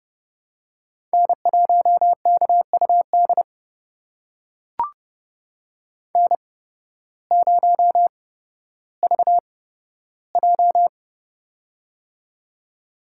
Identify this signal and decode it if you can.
Morse code, heard as a keyed tone, about 30 wpm, spaced as for 14 wpm, 700 Hz. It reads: N1KUD N0VJ
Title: USA Call Signs - 8x Character Spacing - 30wpm